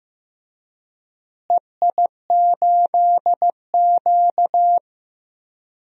Morse code, heard as a keyed tone, about 15 wpm, 700 Hz. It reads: EI8Q
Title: DXCC-100 Call Signs - 4 Characters - Repeated 2x - 15wpm